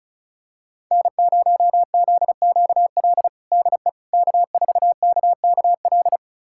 Morse code, heard as a keyed tone, about 36 wpm, 700 Hz. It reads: N0ZQL DE K4KKL
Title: USA Callsign DE Callsign - 5 Characters - 35wpm